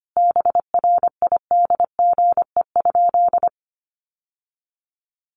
Morse code, heard as a keyed tone, about 25 wpm, 700 Hz. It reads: BRIDGE?
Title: Sentences from Top 1200 Words 25wpm